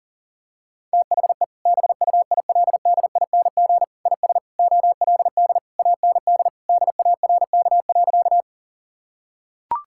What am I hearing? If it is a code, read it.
Morse code, heard as a keyed tone, about 40 wpm, 700 Hz. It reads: THE BUILDING IS OLD AND DARK.
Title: Sentences from Top 1000 Words - Review Part 5 - 40wpm